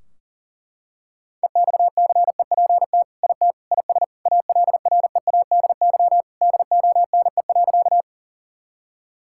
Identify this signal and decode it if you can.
Morse code, heard as a keyed tone, about 37 wpm, 700 Hz. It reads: EXCEPT IT IS ALREADY DONE.
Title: Sentences from Top 1600 Words 40wpm